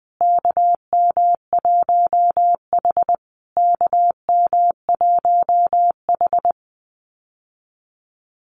Morse code, heard as a keyed tone, about 20 wpm, 700 Hz. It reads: KM1H KM1H
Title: USA Call Signs Repeated 3x 20wpm